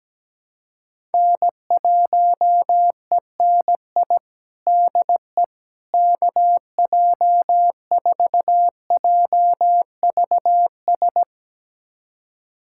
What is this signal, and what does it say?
Morse code, heard as a keyed tone, about 17 wpm, 700 Hz. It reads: N1ENI DE KJ4JVS
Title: USA Callsign DE Callsign 17wpm